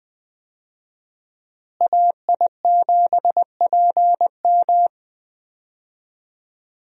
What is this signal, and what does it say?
Morse code, heard as a keyed tone, about 20 wpm, 700 Hz. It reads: AI7PM
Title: USA Call Signs - 5 Characters - Repeated 3x - 20wpm